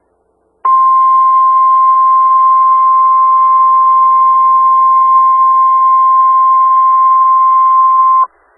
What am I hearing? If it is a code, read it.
Emergency Alert System (SAME) tones (the 1050 Hz attention tone). It is heard from 0.6 to 8.3 s.
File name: Skywarn/2013/2013-03-18-Newton County WA4ASI Repeater